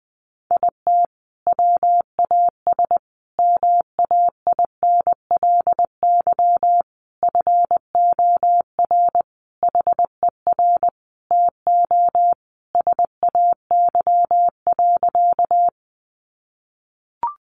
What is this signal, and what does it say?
Morse code, heard as a keyed tone, about 20 wpm, 700 Hz. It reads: IT WAS MAINLY FOR HER TO SAY.
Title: Sentences from Top 1400 Words 20wpm